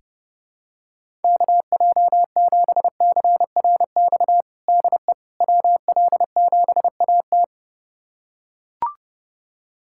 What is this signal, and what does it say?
Morse code, heard as a keyed tone, about 29 wpm, 700 Hz. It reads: KJ7CRX DE WL7AT
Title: USA Callsign DE Callsign 30wpm